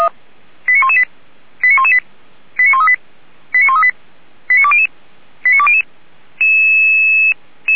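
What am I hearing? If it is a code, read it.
Five-tone selective call (ZVEI), heard as quick runs of five tones, tones of about 70 ms. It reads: 89118 89118 89138 89138 89200 89200
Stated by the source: ZVEI five-tone selective call received on an SDR, from the feuerwehr-orlamuende.org (Orlamünde fire brigade) server.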